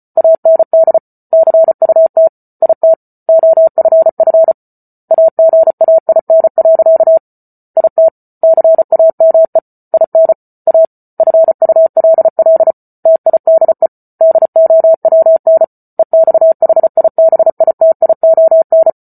Morse code, heard as a keyed tone, about 34 wpm, 650 Hz. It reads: AND CUT IT OFF AGAIN. IT CAME IN A FULL TIDE DOWN EXHIBITION